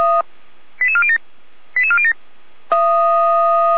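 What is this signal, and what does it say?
Five-tone selective call (ZVEI), heard as quick runs of five tones, tones of about 70 ms. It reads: 88407 88407
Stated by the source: ZVEI five-tone selective call received on SDR.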